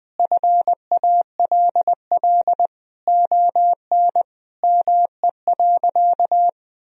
Morse code, heard as a keyed tone, about 20 wpm, 700 Hz. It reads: FALL ON ME.
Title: Sentences from Top 1000 Words - Review Part 2 - 20wpm